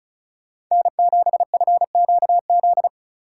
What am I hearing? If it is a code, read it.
Morse code, heard as a keyed tone, about 36 wpm, 700 Hz. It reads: N7FQZ
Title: USA Call Signs - 2x Character Spacing - 35wpm